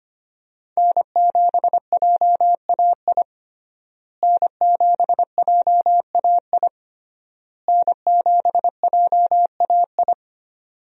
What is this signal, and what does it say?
Morse code, heard as a keyed tone, about 25 wpm, 700 Hz. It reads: N7JAI N7JAI N7JAI
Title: USA Call Signs - 5 Characters - Repeated 3x - 25wpm